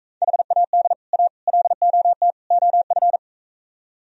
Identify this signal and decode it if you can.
Morse code, heard as a keyed tone, about 45 wpm, 700 Hz. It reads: HAD A LOT OF